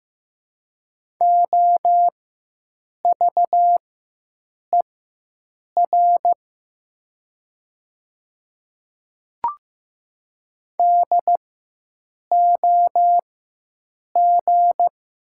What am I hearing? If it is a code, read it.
Morse code, heard as a keyed tone, about 15 wpm, spaced as for 7 wpm, 700 Hz. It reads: OVER DOG
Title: 4x Character Spacing - Top 500 Words - 15wpm